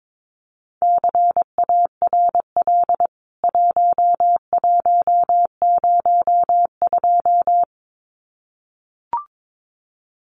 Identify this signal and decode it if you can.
Morse code, heard as a keyed tone, about 22 wpm, 700 Hz. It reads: CARL 1102
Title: CWT Exchange 22wpm